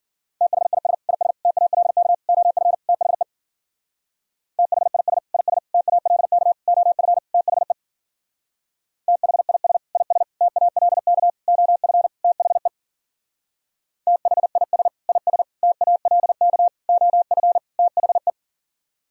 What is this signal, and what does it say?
Morse code, heard as a keyed tone, about 48 wpm, 700 Hz. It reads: THIS IS TALK OF THE THIS IS TALK OF THE THIS IS TALK OF THE THIS IS TALK OF THE
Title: Speed-Racing - Sets of 5 Words 40wpm